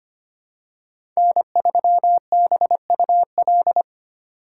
Morse code, heard as a keyed tone, about 25 wpm, 700 Hz. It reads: N3BUL
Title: USA Call Signs - 2x Character Spacing - 25wpm